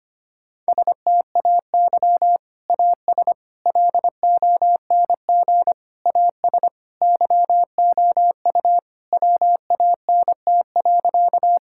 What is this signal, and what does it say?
Morse code, heard as a keyed tone, about 25 wpm, 700 Hz. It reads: STAY AS LONG AS YOU WANT.
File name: practice / 25wpm / Sentences with 3-Word Phrases - 25wpm